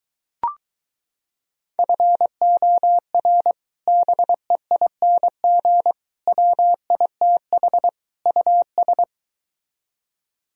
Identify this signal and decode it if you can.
Morse code, heard as a keyed tone, about 24 wpm, 700 Hz. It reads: FOR BEING WITH US